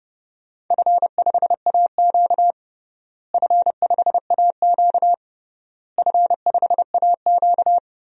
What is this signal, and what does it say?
Morse code, heard as a keyed tone, about 30 wpm, 700 Hz. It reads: F5AQ F5AQ F5AQ
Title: DXCC-100 Call Signs - 4 Characters - Repeated 3x - 30wpm